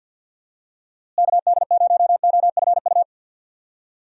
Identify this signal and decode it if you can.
Morse code, heard as a keyed tone, about 48 wpm, 700 Hz. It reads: KD0OFU
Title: USA Call Signs - 6 Characters - Repeated 2x - 50wpm